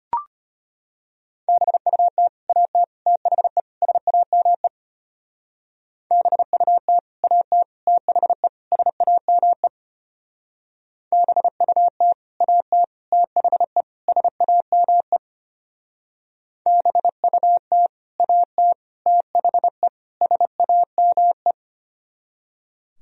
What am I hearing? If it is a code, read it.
Morse code, heard as a keyed tone, about 29 wpm, 700 Hz. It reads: BUT AT THE SAME BUT AT THE SAME BUT AT THE SAME BUT AT THE SAME
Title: Speed-Racing - Sets of 4 Words 25wpm